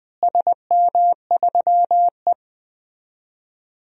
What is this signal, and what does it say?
Morse code, heard as a keyed tone, about 21 wpm, 700 Hz. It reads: SM3E
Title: DXCC-100 Call Signs - 4 Characters - Repeated 3x - 20wpm